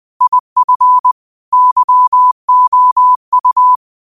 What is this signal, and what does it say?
Morse code, heard as a keyed tone, about 20 wpm, 1000 Hz. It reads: IF YOU
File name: qotdfiles / 20wpm